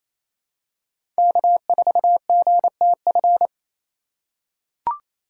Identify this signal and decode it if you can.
Morse code, heard as a keyed tone, about 28 wpm, 700 Hz. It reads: K4GTF
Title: USA Call Signs - 2x Character Spacing - 28wpm